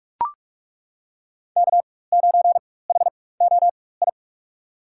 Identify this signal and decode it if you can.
Morse code, heard as a keyed tone, about 42 wpm, spaced as for 21 wpm, 700 Hz. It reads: K9HOI
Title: USA Call Signs - 4x Character Spacing - 45wpm